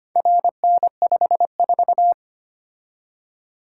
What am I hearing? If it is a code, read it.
Morse code, heard as a keyed tone, about 25 wpm, 700 Hz. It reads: RN54
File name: practice / 25wpm / Four Letter-Number Combinations - 25wpm